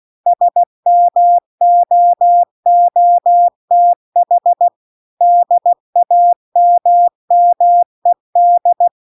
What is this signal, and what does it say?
Morse code, heard as a keyed tone, about 16 wpm, 700 Hz. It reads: SMOOTH DAMMED
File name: CW-The-Big-Twoheaded-River-2-16WPM